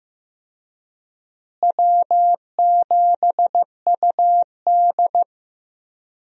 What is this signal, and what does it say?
Morse code, heard as a keyed tone, about 16 wpm, 700 Hz. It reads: W7UD
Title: USA Call Signs - 4 Characters - Repeated 2x - 15wpm